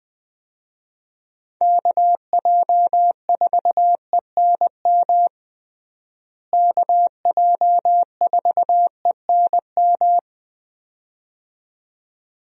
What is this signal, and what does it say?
Morse code, heard as a keyed tone, about 21 wpm, 700 Hz. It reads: KJ4ENM KJ4ENM
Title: USA Call Signs - 6 Characters - Repeated 2x - 20wpm